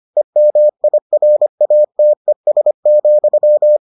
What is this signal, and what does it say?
Morse code, heard as a keyed tone, about 25 wpm, 600 Hz. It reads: EMIRATES,